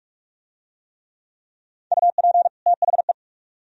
Morse code, heard as a keyed tone, about 43 wpm, 700 Hz. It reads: UP THE